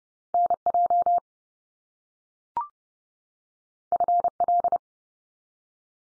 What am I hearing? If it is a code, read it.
Morse code, heard as a keyed tone, about 30 wpm, 700 Hz. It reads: NJ FL
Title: US States 30WPM